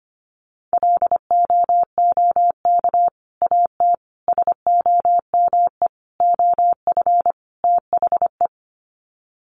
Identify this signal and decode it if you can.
Morse code, heard as a keyed tone, about 25 wpm, 700 Hz. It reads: LOOK AT SOME OF THE
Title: Sets of 5 Words Encore 25wpm